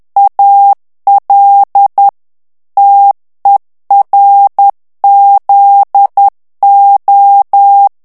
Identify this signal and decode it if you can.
Morse code, heard as a keyed tone, about 10 wpm, 800 Hz. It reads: AL TERZO
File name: ITALIEN texte clair 2 vitesse 10 mots minute